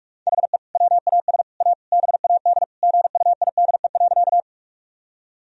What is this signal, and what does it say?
Morse code, heard as a keyed tone, about 43 wpm, 700 Hz. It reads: HE WAS A BAD GUIDE.
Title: Sentences from Top 1000 Words - Review Part 3 - 45wpm